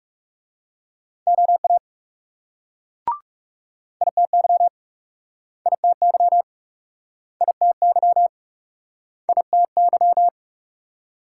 Morse code, heard as a keyed tone, about 41 wpm, 700 Hz. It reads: OA ITY ITY ITY ITY